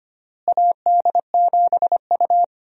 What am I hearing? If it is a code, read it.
Morse code, heard as a keyed tone, about 25 wpm, 700 Hz. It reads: AD7U
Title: USA Call Signs - 4 Characters - Repeated 3x - 25wpm